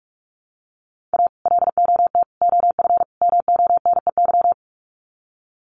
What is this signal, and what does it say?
Morse code, heard as a keyed tone, about 44 wpm, 700 Hz. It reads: A LOT OF MONEY
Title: Sets of 4 Words 45wpm